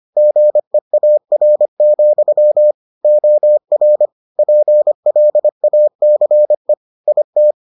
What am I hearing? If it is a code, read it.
Morse code, heard as a keyed tone, about 25 wpm, 600 Hz. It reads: GEAR, OR PLACE IT